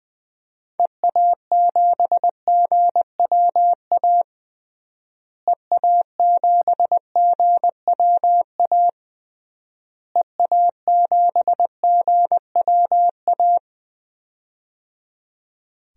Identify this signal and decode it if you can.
Morse code, heard as a keyed tone, about 20 wpm, 700 Hz. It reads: EA7GWA EA7GWA EA7GWA
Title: DXCC-100 Call Signs - 6 Characters - Repeated 3x - 20wpm